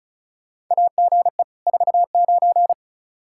Morse code, heard as a keyed tone, about 34 wpm, 700 Hz. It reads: AGE 49
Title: Practice QSO - Exchange 3 - More Info - 35wpm